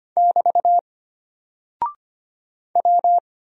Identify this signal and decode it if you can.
Morse code, heard as a keyed tone, about 25 wpm, 700 Hz. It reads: <BT> W
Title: Single Character (Letter:Number:Punctuation:Prosign) Alt 25wpm